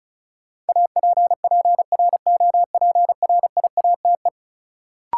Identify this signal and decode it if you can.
Morse code, heard as a keyed tone, about 36 wpm, 700 Hz. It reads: APPROPRIATE
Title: Sentences from Top 900 Words - New Word Drill - 35wpm